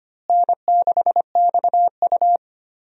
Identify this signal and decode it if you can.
Morse code, heard as a keyed tone, about 25 wpm, 700 Hz. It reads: N6XU
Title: USA Call Signs - 4 Characters - Repeated 3x - 25wpm